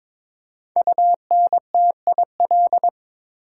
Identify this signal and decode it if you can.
Morse code, heard as a keyed tone, about 23 wpm, 700 Hz. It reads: UNTIL
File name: 5 Letter Words - First 3 Letters Given - 22wpm